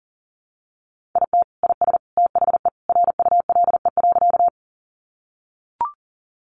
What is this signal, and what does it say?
Morse code, heard as a keyed tone, about 40 wpm, 700 Hz. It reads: IT IS THE RULE.
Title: Sentences from Top 400 Words 40wpm